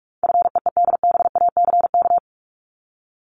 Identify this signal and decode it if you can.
Morse code, heard as a keyed tone, about 44 wpm, 700 Hz. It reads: FEEDBACK
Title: Top 1501-1750 Words 45wpm